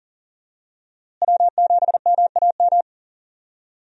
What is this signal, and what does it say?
Morse code, heard as a keyed tone, about 40 wpm, 700 Hz. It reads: W7MAM
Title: USA Call Signs - 5 Characters - Repeated 3x - 40wpm